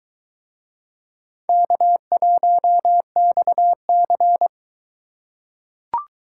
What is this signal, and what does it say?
Morse code, heard as a keyed tone, about 24 wpm, 700 Hz. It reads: K1XC